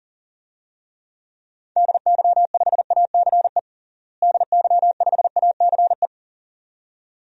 Morse code, heard as a keyed tone, about 40 wpm, 700 Hz. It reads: DY5ACE DY5ACE
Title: DXCC-100 Call Signs - 6 Characters - Repeated 2x - 40wpm